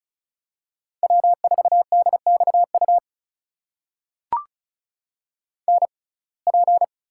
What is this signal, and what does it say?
Morse code, heard as a keyed tone, about 34 wpm, spaced as for 25 wpm, 700 Hz. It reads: W4DXU NP
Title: USA Call Signs - 5 Characters - 6x Character Spacing - 35wpm